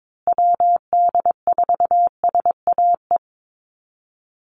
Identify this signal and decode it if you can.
Morse code, heard as a keyed tone, about 22 wpm, 700 Hz. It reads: WD4SAE
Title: USA Call Signs Repeated 3x 22wpm